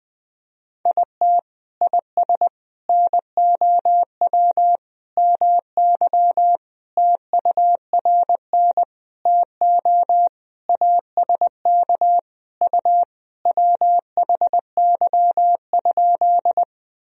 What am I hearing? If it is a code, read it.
Morse code, heard as a keyed tone, about 20 wpm, 700 Hz. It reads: IT IS NOW MY TURN TO ASK U WHY?
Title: Sentences from Top 200 Words and Abbreviations 20wpm